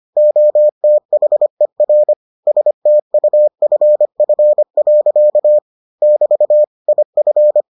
Morse code, heard as a keyed tone, about 25 wpm, 600 Hz. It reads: OTHER STUFF. <BT> IF